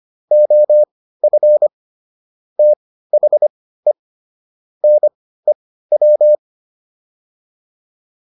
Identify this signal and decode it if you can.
Morse code, heard as a keyed tone, about 25 wpm, spaced as for 15 wpm, 600 Hz. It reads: OF THE NEW
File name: WD204 15